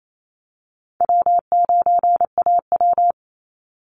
Morse code, heard as a keyed tone, about 28 wpm, 700 Hz. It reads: W9AW
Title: Easy Call Signs Repeated 3x - 28wpm